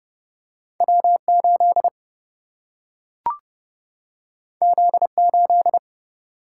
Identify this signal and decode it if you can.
Morse code, heard as a keyed tone, about 30 wpm, 700 Hz. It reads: W8 Z8
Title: Single-Letter Single-Number 30wpm